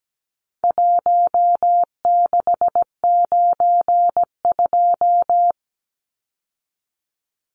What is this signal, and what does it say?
Morse code, heard as a keyed tone, about 17 wpm, 700 Hz. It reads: 1692
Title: Numbers - 4-Digit 17wpm